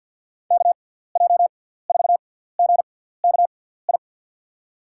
Morse code, heard as a keyed tone, about 53 wpm, spaced as for 17 wpm, 700 Hz. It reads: KJ4GKI
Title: USA Call Signs - 6 Characters - 6x Character Spacing - 50wpm